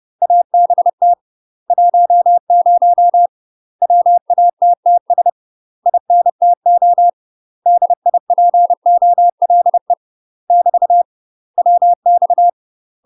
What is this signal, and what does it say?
Morse code, heard as a keyed tone, about 30 wpm, 700 Hz. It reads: ABT 10 WATTS INTO DIPOLE <BT> WX
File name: UK MASTER QSOs TRAINING/30WPM